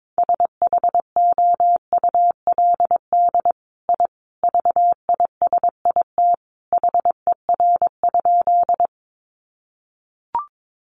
Morse code, heard as a keyed tone, about 22 wpm, 700 Hz. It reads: SHOULD I VISIT HER?
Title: Sentences from Top 900 Words 22wpm